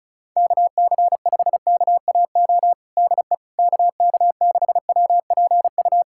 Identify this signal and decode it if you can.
Morse code, heard as a keyed tone, about 34 wpm, 700 Hz. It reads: KC5KAO DE KK6WPU
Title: USA Callsign DE Callsign 35wpm